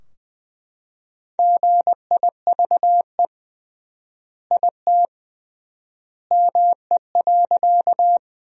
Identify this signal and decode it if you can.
Morse code, heard as a keyed tone, about 21 wpm, 700 Hz. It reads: GIVE IT ME.
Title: 3x Word-Spacing - Sentences from Top 200 Words - 20wpm